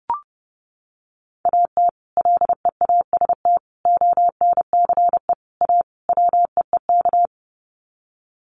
Morse code, heard as a keyed tone, about 28 wpm, 700 Hz. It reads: AT LEAST ONCE A WEEK
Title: Sets of 5 Words Encore2 30wpm